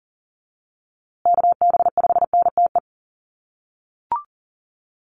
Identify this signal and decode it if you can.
Morse code, heard as a keyed tone, about 40 wpm, 700 Hz. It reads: KB5NTE
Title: USA Call Signs Repeated 3x 40wpm